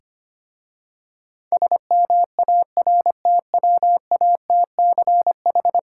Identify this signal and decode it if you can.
Morse code, heard as a keyed tone, about 25 wpm, 700 Hz. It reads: SMARTWATCH
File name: practice / 25wpm / Difficult Words with Hints - 25wpm